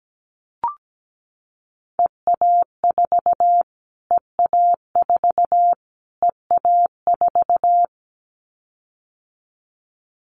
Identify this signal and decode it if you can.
Morse code, heard as a keyed tone, about 17 wpm, 700 Hz. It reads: EA4 EA4 EA4
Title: Callsign Prefixes for Top 25 DXCC - Repeated 3x - 17wpm